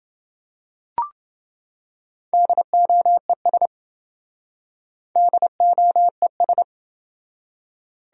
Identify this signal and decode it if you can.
Morse code, heard as a keyed tone, about 27 wpm, 700 Hz. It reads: DOES DOES